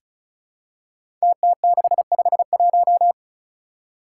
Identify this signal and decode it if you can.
Morse code, heard as a keyed tone, about 36 wpm, 700 Hz. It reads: TT651
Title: Five Challenging Letter-Number Combinations 35wpm